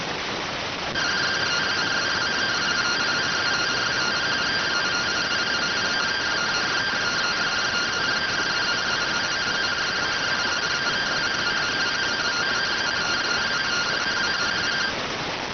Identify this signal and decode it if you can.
Two-tone teleprinter signal, about 170 Hz shift.